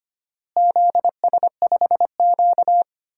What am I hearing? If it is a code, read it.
Morse code, heard as a keyed tone, about 25 wpm, 700 Hz. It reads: ZS5Q
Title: DXCC-100 Call Signs - 4 Characters - Repeated 2x - 25wpm